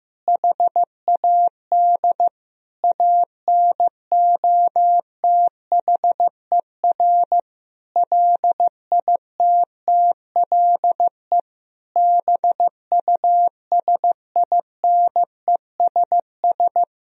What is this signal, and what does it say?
Morse code, heard as a keyed tone, about 15 wpm, 700 Hz. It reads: HAD ANOTHER LITTLE BUSINESS
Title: Sentences from Top 1000 Words - Review Part 1 - 15wpm